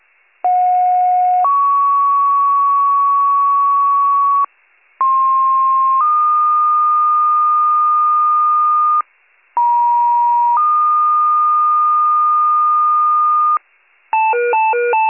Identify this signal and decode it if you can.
Two-tone page, heard as a one-second tone, then a three-second tone, about 730 then 1120 Hz, about 1060 then 1220 Hz, about 950 then 1220 Hz. It is heard from 0.4 to 13.6 s.